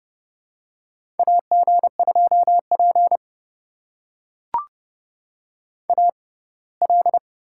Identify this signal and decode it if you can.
Morse code, heard as a keyed tone, about 30 wpm, spaced as for 20 wpm, 700 Hz. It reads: AG2P AL
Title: USA Call Signs - 4 Characters - 6x Character Spacing - 30wpm